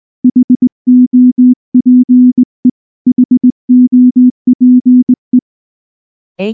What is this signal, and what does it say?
Morse code, heard as a keyed tone, about 19 wpm, 250 Hz. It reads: HOPE HOPE
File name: morse / morse-training-20WPM-1